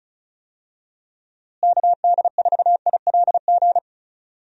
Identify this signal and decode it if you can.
Morse code, heard as a keyed tone, about 36 wpm, 700 Hz. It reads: KD4ILG
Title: USA Call Signs - 6 Characters - Repeated 2x - 35wpm